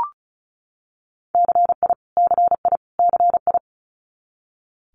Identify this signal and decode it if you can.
Morse code, heard as a keyed tone, about 36 wpm, 700 Hz. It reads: CI CI CI
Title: Multi-letter Phonemes Repeated 3x 35wpm